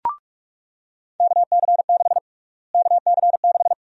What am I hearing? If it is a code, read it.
Morse code, heard as a keyed tone, about 46 wpm, 700 Hz. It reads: KC6 KC6
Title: USA Call Sign Prefixes - Repeated 2x - 45wpm